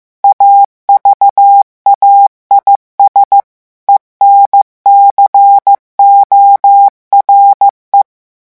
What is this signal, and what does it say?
Morse code, heard as a keyed tone, about 15 wpm, 800 Hz. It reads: AVAIS ENCORE
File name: morse mp3 / revision9 citations humoristiques vitesse 14 mots minute